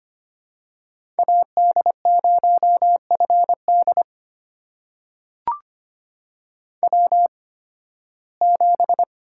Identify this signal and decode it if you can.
Morse code, heard as a keyed tone, about 25 wpm, spaced as for 16 wpm, 700 Hz. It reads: AD0FD W7
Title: USA Call Signs - 8x Character Spacing - 25wpm